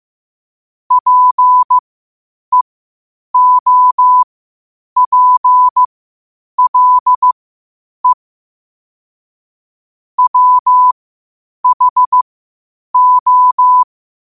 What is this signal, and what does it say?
Morse code, heard as a keyed tone, about 15 wpm, spaced as for 8 wpm, 1000 Hz. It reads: PEOPLE WHO
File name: qotdfiles / 8wpm